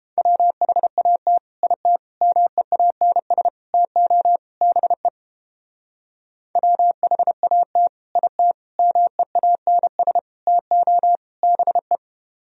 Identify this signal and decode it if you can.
Morse code, heard as a keyed tone, about 29 wpm, 700 Hz. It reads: WHAT IT MEANS TO BE WHAT IT MEANS TO BE